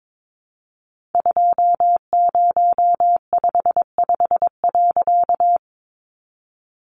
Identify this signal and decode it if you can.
Morse code, heard as a keyed tone, about 22 wpm, 700 Hz. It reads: 2055.
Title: A Sound Of Thunder 22wpm